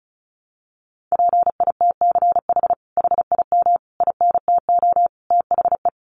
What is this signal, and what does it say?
Morse code, heard as a keyed tone, about 34 wpm, 700 Hz. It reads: PITCH HIM INTO THE
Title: Sentences from Top 1300 Words 35wpm